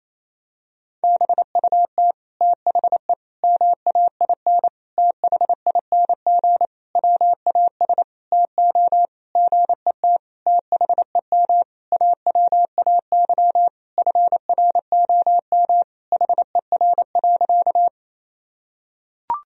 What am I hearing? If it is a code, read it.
Morse code, heard as a keyed tone, about 27 wpm, 700 Hz. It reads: BUT THE MAIN THING WAS TO GET THEM AWAY FROM HER.
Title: Sentences from Top 1000 Words - Review Part 2 - 28wpm